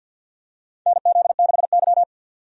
Morse code, heard as a keyed tone, about 52 wpm, 700 Hz. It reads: NZ6Y
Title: USA Call Signs - 4 Characters - 50wpm